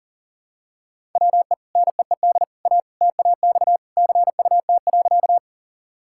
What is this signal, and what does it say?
Morse code, heard as a keyed tone, about 40 wpm, 700 Hz. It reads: WE NEED A TAX CUT.
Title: Sentences from Top 1000 Words - Review Part 5 - 40wpm